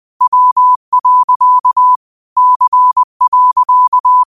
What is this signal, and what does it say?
Morse code, heard as a keyed tone, about 20 wpm, 1000 Hz. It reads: W. C.
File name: qotdfiles / 20wpm